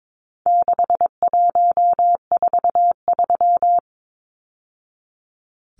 Morse code, heard as a keyed tone, about 22 wpm, 700 Hz. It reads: 6143
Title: Numbers - 4-Digit 22wpm